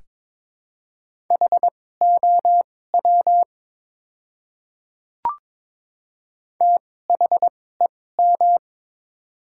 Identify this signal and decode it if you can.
Morse code, heard as a keyed tone, about 23 wpm, spaced as for 16 wpm, 690 Hz. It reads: HOW THEM
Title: 2x Character Spacing - Top 100 Words - 22wpm